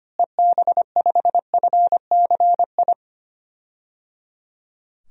Morse code, heard as a keyed tone, about 25 wpm, 700 Hz. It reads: EB5FCI
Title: DXCC-100 Call Signs - 6 Characters - Repeated 3x - 25wpm